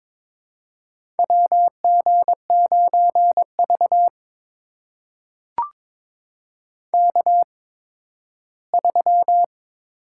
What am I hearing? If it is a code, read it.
Morse code, heard as a keyed tone, about 22 wpm, spaced as for 13 wpm, 700 Hz. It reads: WG9V K3
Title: USA Call Signs - 4 Characters - 8x Character Spacing - 22wpm